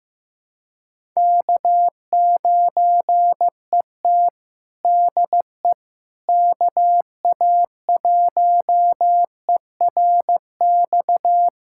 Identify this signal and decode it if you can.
Morse code, heard as a keyed tone, about 15 wpm, 700 Hz. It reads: K9ET DE KA1ERX
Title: USA Callsign DE Callsign 15wpm